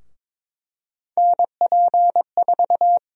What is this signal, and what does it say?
Morse code, heard as a keyed tone, about 21 wpm, 700 Hz. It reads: NP4
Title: USA Call Sign Prefixes - 22wpm